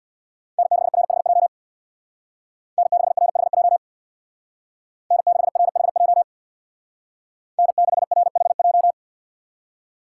Morse code, heard as a keyed tone, about 62 wpm, 700 Hz. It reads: N6RHJ N6RHJ N6RHJ N6RHJ
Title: Speed Racing - USA Call Signs 50wpm